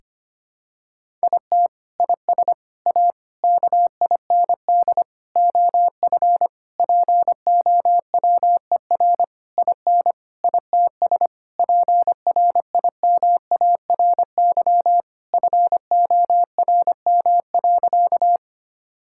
Morse code, heard as a keyed tone, about 25 wpm, 700 Hz. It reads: IT IS A KIND OF POWER IN ITS PRIMARY FORM.
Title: Sentences from Top 1000 Words - Review Part 5 - 25wpm